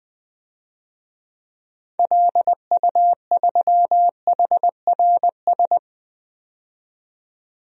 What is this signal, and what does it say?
Morse code, heard as a keyed tone, about 20 wpm, 700 Hz. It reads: LU3HRS
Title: DXCC-100 Call Signs - 6 Characters - 20wpm